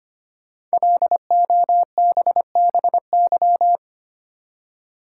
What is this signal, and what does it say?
Morse code, heard as a keyed tone, about 25 wpm, 700 Hz. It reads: LOBBY
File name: 5 Letter Words - First 3 Letters Given - 25wpm